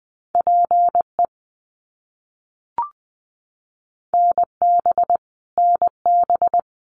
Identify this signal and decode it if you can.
Morse code, heard as a keyed tone, about 20 wpm, 700 Hz. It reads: PE NB NB
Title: Canadian Provinces and Territories - Repeated 3x - 20wpm